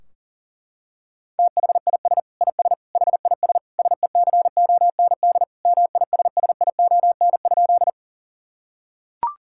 Morse code, heard as a keyed tone, about 37 wpm, 700 Hz. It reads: THIS IS HIS SECOND MISSION?
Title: Sentences from Top 1600 Words 40wpm